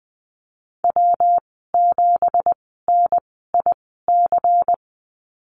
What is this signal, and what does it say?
Morse code, heard as a keyed tone, about 20 wpm, spaced as for 15 wpm, 700 Hz. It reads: W7NIC
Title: 193 Lesson 31g - Letter v - Call Signs 20wpm